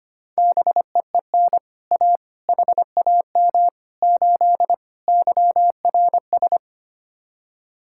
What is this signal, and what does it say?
Morse code, heard as a keyed tone, about 32 wpm, 700 Hz. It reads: BEEN A HAM 8 YRS